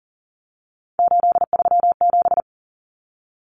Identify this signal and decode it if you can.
Morse code, heard as a keyed tone, about 40 wpm, 700 Hz. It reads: 837
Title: Numbers - 3-Digit 40wpm